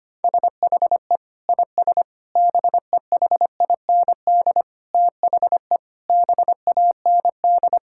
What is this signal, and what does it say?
Morse code, heard as a keyed tone, about 25 wpm, 700 Hz. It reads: SHE IS BEHIND THE BAND
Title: Sentences from Top 1600 Words 25wpm